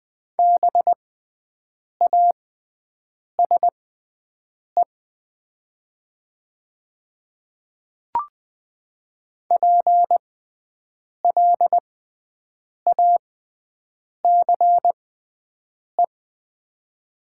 Morse code, heard as a keyed tone, about 20 wpm, spaced as for 7 wpm, 700 Hz. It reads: BASE PLACE
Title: 6x Character Spacing - Top 500 Words - 20wpm